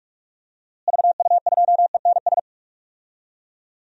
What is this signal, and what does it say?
Morse code, heard as a keyed tone, about 43 wpm, 700 Hz. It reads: VU2ENS
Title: DXCC-100 Call Signs - 6 Characters - Repeated 3x - 45wpm